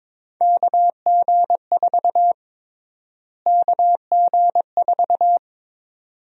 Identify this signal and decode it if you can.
Morse code, heard as a keyed tone, about 22 wpm, 700 Hz. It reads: KG4 KG4
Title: USA Call Sign Prefixes - Repeated 2x - 22wpm